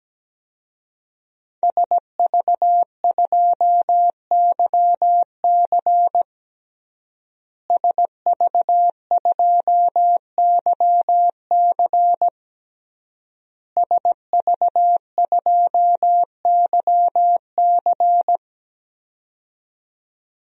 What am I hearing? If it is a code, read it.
Morse code, heard as a keyed tone, about 17 wpm, 700 Hz. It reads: SV2YC SV2YC SV2YC
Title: DXCC-100 Call Signs - 5 Characters - Repeated 3x - 17wpm